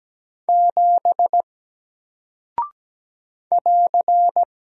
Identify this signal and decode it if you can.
Morse code, heard as a keyed tone, about 17 wpm, 700 Hz. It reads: 7 <AR>
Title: Single Character (Letter:Number:Punctuation:Prosign) Alt 17wpm